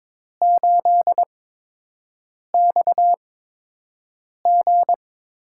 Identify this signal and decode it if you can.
Morse code, heard as a keyed tone, about 22 wpm, spaced as for 6 wpm, 700 Hz. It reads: 8XG
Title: USA Call Signs - 4 Characters - 8x Character Spacing - 22wpm